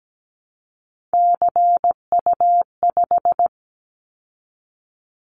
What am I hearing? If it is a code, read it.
Morse code, heard as a keyed tone, about 17 wpm, 700 Hz. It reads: CU5
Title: Callsign Prefixes for Top 50 DXCC - 17wpm